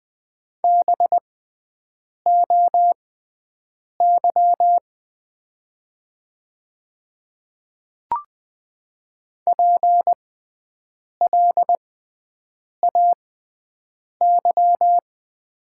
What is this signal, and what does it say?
Morse code, heard as a keyed tone, about 20 wpm, spaced as for 7 wpm, 700 Hz. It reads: BOY PLAY
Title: 6x Character Spacing - Top 500 Words - 20wpm